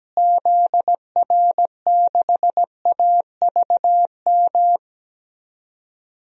Morse code, heard as a keyed tone, about 17 wpm, 700 Hz. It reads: ZR6AVM
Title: DXCC-100 Call Signs - 6 Characters - Repeated 3x - 17wpm